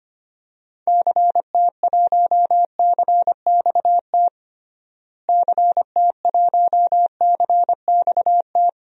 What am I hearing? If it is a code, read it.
Morse code, heard as a keyed tone, about 25 wpm, 700 Hz. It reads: CT1CXT CT1CXT
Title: DXCC-100 Call Signs - 6 Characters - Repeated 3x - 25wpm